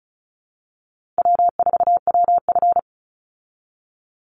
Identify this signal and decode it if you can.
Morse code, heard as a keyed tone, about 34 wpm, 700 Hz. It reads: W4WF
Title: Easy Call Signs Repeated 3x - 35wpm